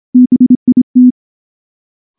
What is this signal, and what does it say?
Morse code, heard as a keyed tone, about 23 wpm, 250 Hz. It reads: BIT
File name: morse / morse-training-25WPM